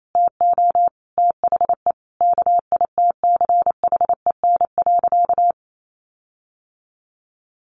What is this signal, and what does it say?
Morse code, heard as a keyed tone, about 28 wpm, 700 Hz. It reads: TO THE KITCHEN.
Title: Sentences from Top 1400 Words 28wpm